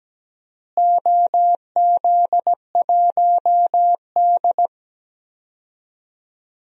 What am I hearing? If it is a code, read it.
Morse code, heard as a keyed tone, about 17 wpm, 700 Hz. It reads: OZ1D
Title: DXCC-100 Call Signs - 4 Characters - Repeated 2x - 17wpm